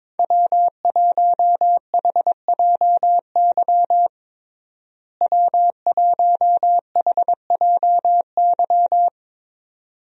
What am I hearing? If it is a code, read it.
Morse code, heard as a keyed tone, about 22 wpm, 700 Hz. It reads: W1HJY W1HJY
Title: USA Call Signs - 5 Characters - Repeated 2x - 22wpm